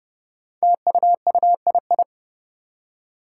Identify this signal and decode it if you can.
Morse code, heard as a keyed tone, about 30 wpm, 700 Hz. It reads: TUUII
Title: Five Challenging Letter-Number Combinations 30wpm